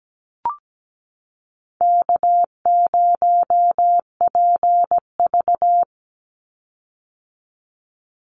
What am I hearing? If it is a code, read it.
Morse code, heard as a keyed tone, about 17 wpm, 700 Hz. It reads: K0PV
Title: Easy Call Signs - 17wpm